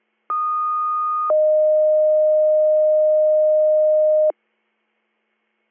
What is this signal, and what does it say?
Two-tone page, heard as a one-second tone, then a three-second tone, about 1230 then 620 Hz. It is heard from 0.3 to 4.3 s.